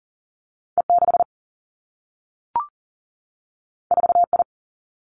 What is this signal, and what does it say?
Morse code, heard as a keyed tone, about 38 wpm, 700 Hz. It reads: E 6 4 I
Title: Two Challenging Letter-Number Combinations 40wpm